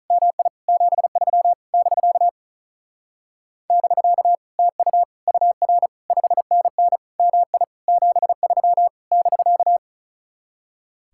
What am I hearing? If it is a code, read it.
Morse code, heard as a keyed tone, about 46 wpm, 700 Hz. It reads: MI 73 <BK> <BK> TU UR 5NN MI 73 <BK>